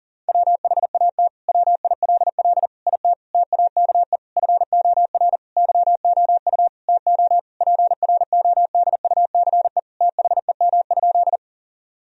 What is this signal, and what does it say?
Morse code, heard as a keyed tone, about 40 wpm, 700 Hz. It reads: WHAT WILL IT TAKE FOR YOU TO PRODUCE THEM?
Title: Sentences from Top 1000 Words - Review Part 3 - 40wpm